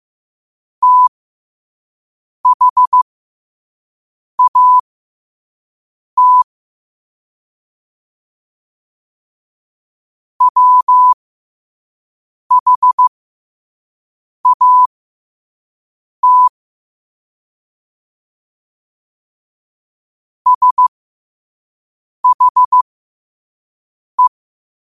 Morse code, heard as a keyed tone, about 15 wpm, spaced as for 5 wpm, 1000 Hz. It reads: THAT WHAT SHE
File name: qotdfiles / 5wpm